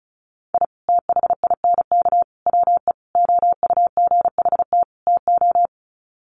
Morse code, heard as a keyed tone, about 34 wpm, 700 Hz. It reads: I THINK WE OUGHT TO
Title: Sets of 5 Words Encore 35wpm